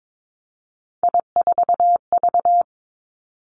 Morse code, heard as a keyed tone, about 22 wpm, 700 Hz. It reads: I4V
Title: Three Challenging Letter-Number Combinations 22wpm